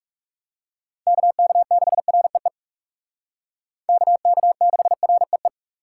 Morse code, heard as a keyed tone, about 38 wpm, 700 Hz. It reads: KK6REE KK6REE